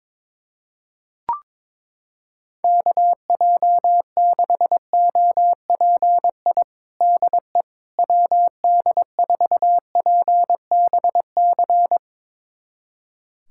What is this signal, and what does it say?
Morse code, heard as a keyed tone, about 22 wpm, 700 Hz. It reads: KJ6OPI DE WD4PBC
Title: USA Callsign DE Callsign - 6 Characters - 22wpm